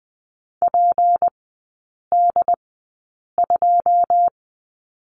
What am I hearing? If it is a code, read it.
Morse code, heard as a keyed tone, about 20 wpm, spaced as for 8 wpm, 700 Hz. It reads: PD2
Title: 178 Lesson 29e - Slash - Three Characters 20wpm